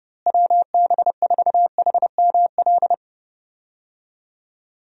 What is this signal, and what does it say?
Morse code, heard as a keyed tone, about 30 wpm, 700 Hz. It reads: WB4HML
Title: USA Call Signs - 6 Characters - Repeated 2x - 30wpm